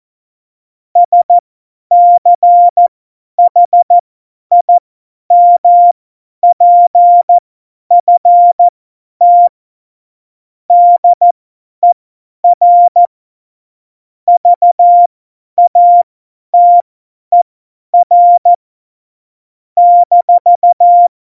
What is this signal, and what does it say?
Morse code, heard as a keyed tone, about 14 wpm, spaced as for 10 wpm, 700 Hz. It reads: SCHIMPFT DER VATER -
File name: Morsekurs / Witze / Witz 31. Oktober 17, Tempo 50